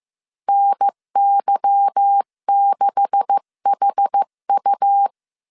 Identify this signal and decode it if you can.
Morse code, heard as a keyed tone, about 15 wpm, 800 Hz. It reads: NY6HU
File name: CW / DET / 1 9 1500C S B 14wpm 3302